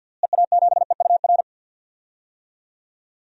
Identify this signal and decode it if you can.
Morse code, heard as a keyed tone, about 52 wpm, 700 Hz. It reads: EA7EUR
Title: DXCC-100 Call Signs - 6 Characters - Repeated 2x - 50wpm